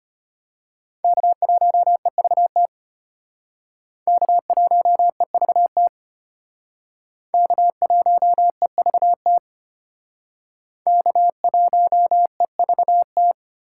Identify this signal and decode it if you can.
Morse code, heard as a keyed tone, about 31 wpm, 700 Hz. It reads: K1EVT K1EVT K1EVT K1EVT
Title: Speed Racing - USA 1x3 Call Signs 25wpm